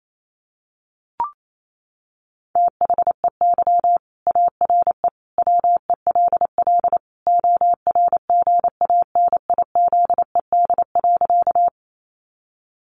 Morse code, heard as a keyed tone, about 28 wpm, 700 Hz. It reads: THEY ARE WELL ORGANIZED.
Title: Sentences from Top 1300 Words 28wpm